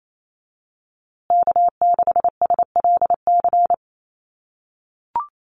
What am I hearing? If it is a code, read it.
Morse code, heard as a keyed tone, about 28 wpm, 700 Hz. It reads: K6SLC
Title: USA Call Signs Repeated 2x 28wpm